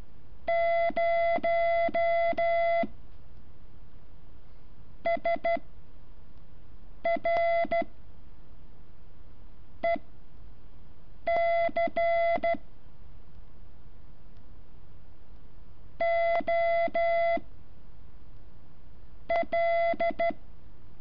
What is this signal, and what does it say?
Morse code, heard as a keyed tone, about 13 wpm, 680 Hz. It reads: 0 S R E C O L
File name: curso / ejercicio 14